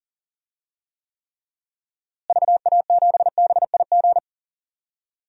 Morse code, heard as a keyed tone, about 40 wpm, 700 Hz. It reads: VA7BIG
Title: DXCC-100 Call Signs - 6 Characters - Repeated 3x - 40wpm